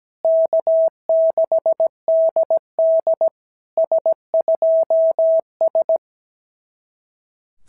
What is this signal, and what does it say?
Morse code, heard as a keyed tone, about 17 wpm, 650 Hz. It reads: K6DD S2S
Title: SOTA QSO Practice 17wpm